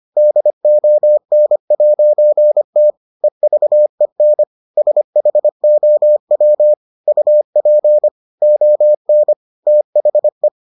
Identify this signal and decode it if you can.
Morse code, heard as a keyed tone, about 25 wpm, 600 Hz. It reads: DON'T EVEN SHOW UP ON THE